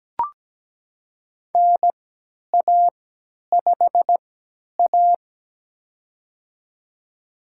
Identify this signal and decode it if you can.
Morse code, heard as a keyed tone, about 17 wpm, spaced as for 10 wpm, 700 Hz. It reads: NA5A
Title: USA Call Signs - 4 Characters - 3x Character Spacing - 17wpm